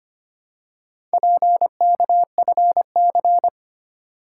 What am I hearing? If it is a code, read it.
Morse code, heard as a keyed tone, about 25 wpm, 690 Hz. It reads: PKFC
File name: practice / 25wpm / Four Letter-Number Combinations - 25wpm